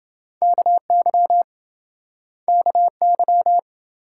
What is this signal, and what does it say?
Morse code, heard as a keyed tone, about 27 wpm, 700 Hz. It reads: KY KY